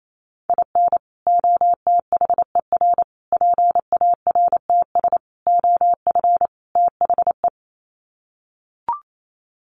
Morse code, heard as a keyed tone, about 28 wpm, 700 Hz. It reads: IN OTHER PARTS OF THE
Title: Sets of 5 Words 28wpm